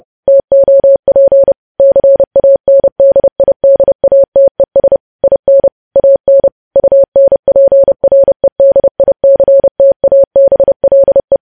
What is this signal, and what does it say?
Morse code, heard as a keyed tone, about 30 wpm, 550 Hz. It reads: TOP CANDIDATES IN AN UNPREDICTABLE
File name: podcast / News headlines-1x-30x30